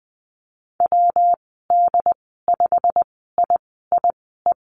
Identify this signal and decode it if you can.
Morse code, heard as a keyed tone, about 19 wpm, spaced as for 14 wpm, 700 Hz. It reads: WD5IIE
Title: 128 Lesson 21g - Number 6 - Call Signs 20wpm